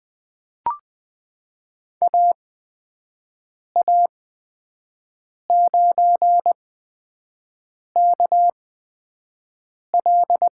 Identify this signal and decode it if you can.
Morse code, heard as a keyed tone, about 20 wpm, spaced as for 5 wpm, 700 Hz. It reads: AA9KL
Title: USA Call Signs - 5 Characters - 8x Character Spacing - 20wpm